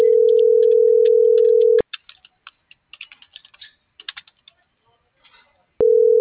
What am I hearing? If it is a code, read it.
Ringback tone (North America), 440 + 480 Hz.